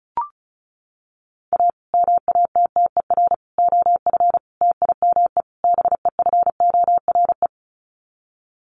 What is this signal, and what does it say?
Morse code, heard as a keyed tone, about 34 wpm, 700 Hz. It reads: A MATTER OF TIME BEFORE
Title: Sets of 5 Words Encore 35wpm